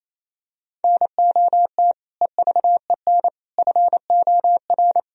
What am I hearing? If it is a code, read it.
Morse code, heard as a keyed tone, about 28 wpm, 700 Hz. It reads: NOT EVEN FOR
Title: Sentences from Top 1000 Words - Review Part 3 - 28wpm